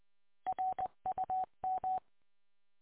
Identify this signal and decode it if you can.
Morse code, heard as a keyed tone, about 20 wpm, 750 Hz. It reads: RUM